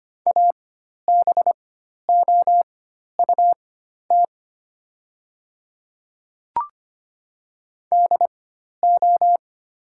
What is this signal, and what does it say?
Morse code, heard as a keyed tone, about 25 wpm, spaced as for 12 wpm, 700 Hz. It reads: ABOUT DO
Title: 4x Character Spacing - Top 100 Words - 25wpm